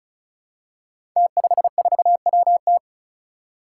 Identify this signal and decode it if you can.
Morse code, heard as a keyed tone, about 36 wpm, 700 Hz. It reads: T54WT
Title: Five Challenging Letter-Number Combinations 35wpm